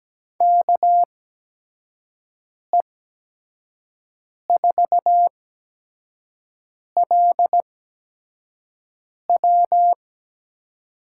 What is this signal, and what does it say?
Morse code, heard as a keyed tone, about 17 wpm, spaced as for 5 wpm, 700 Hz. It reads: KE4LW
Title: USA Call Signs - 6 Characters - 8x Character Spacing - 17wpm